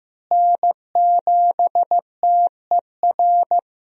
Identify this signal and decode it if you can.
Morse code, heard as a keyed tone, about 15 wpm, 700 Hz. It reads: N7TER
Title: USA Call Signs - 5 Characters - Repeated 3x - 15wpm